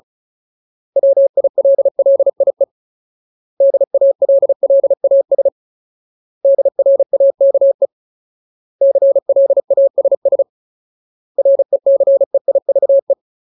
Morse code, heard as a keyed tone, about 35 wpm, 550 Hz. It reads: WILLIE DALLAS DRAKE CLASS RECEIVE
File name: Ultimate Word List / 35WPM Ultimate